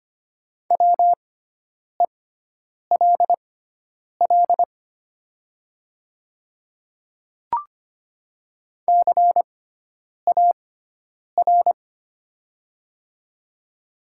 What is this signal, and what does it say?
Morse code, heard as a keyed tone, about 25 wpm, spaced as for 9 wpm, 700 Hz. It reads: WELL CAR
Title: 6x Character Spacing - Top 300 Words - 25wpm